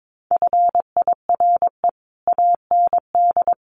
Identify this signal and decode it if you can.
Morse code, heard as a keyed tone, about 22 wpm, 700 Hz. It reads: FIRE AND
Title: Sentences from Top 400 Words 22wpm